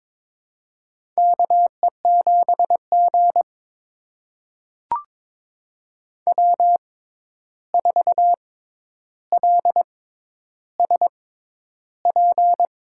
Morse code, heard as a keyed tone, about 22 wpm, spaced as for 11 wpm, 700 Hz. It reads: KE7G W4LSP
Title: USA Call Signs - 6x Character Spacing - 22wpm